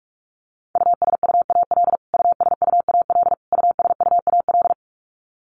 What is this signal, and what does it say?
Morse code, heard as a keyed tone, about 44 wpm, 700 Hz. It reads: USUAL USUAL USUAL
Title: Top 500 Words Repeated 45wpm